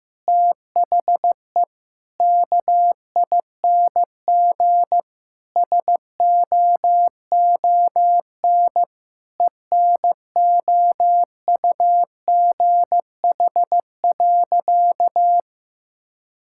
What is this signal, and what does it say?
Morse code, heard as a keyed tone, about 15 wpm, 700 Hz. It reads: THE KING SOON ENOUGH.
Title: Sentences from Top 1000 Words - Review Part 6 - 15wpm